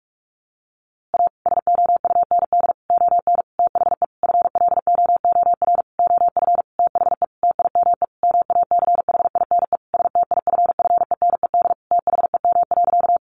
Morse code, heard as a keyed tone, about 44 wpm, 700 Hz. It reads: A SOUND ON THE FLOOR OF THE TIME MACHINE STIFFENED THEM.
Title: A Sound Of Thunder 45wpm